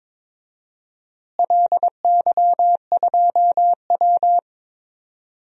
Morse code, heard as a keyed tone, about 22 wpm, 700 Hz. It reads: LY2W
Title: DXCC-100 Call Signs - 4 Characters - Repeated 2x - 22wpm